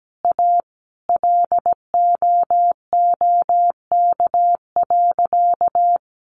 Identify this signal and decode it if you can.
Morse code, heard as a keyed tone, about 17 wpm, 700 Hz. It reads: A LOOK.
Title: A Sound Of Thunder 17wpm